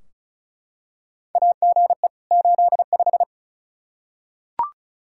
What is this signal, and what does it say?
Morse code, heard as a keyed tone, about 36 wpm, 700 Hz. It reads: AGE 85
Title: Practice QSO - Exchange 3 - More Info - 35wpm